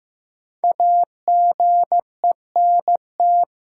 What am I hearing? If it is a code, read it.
Morse code, heard as a keyed tone, about 15 wpm, 700 Hz. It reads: AGENT
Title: Sentences from Top 1600 Words - New Word Drill Alt - 15wpm